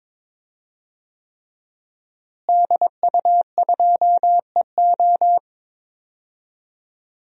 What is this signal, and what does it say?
Morse code, heard as a keyed tone, about 22 wpm, 700 Hz. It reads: DU2EO
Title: DXCC-100 Call Signs - 5 Characters - Repeated 3x - 22wpm